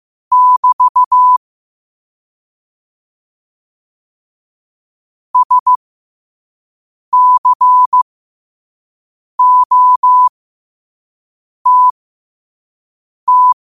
Morse code, heard as a keyed tone, about 15 wpm, spaced as for 5 wpm, 1000 Hz. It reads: <BT> SCOTT
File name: qotdfiles / 5wpm